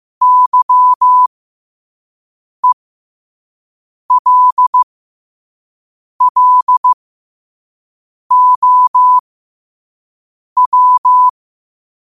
Morse code, heard as a keyed tone, about 15 wpm, spaced as for 5 wpm, 1000 Hz. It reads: YELLOW
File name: qotdfiles / 5wpm